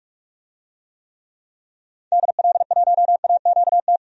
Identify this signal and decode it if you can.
Morse code, heard as a keyed tone, about 46 wpm, 700 Hz. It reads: DL1AQT
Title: DXCC-100 Call Signs - 6 Characters - 45wpm